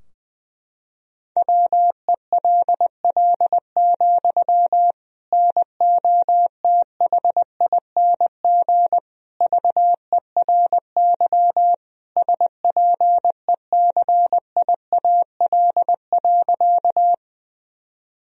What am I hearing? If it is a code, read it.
Morse code, heard as a keyed tone, about 21 wpm, 700 Hz. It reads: WELL, NOTHING VERY SPECIAL.
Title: Sentences from Top 1000 Words - Review Part 3 - 20wpm